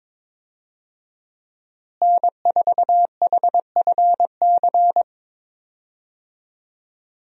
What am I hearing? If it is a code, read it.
Morse code, heard as a keyed tone, about 22 wpm, 700 Hz. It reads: N4HFC
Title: USA Call Signs - 5 Characters - Repeated 3x - 22wpm